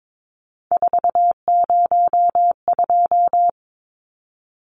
Morse code, heard as a keyed tone, about 22 wpm, 700 Hz. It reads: 402
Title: Numbers - 3-Digit 22wpm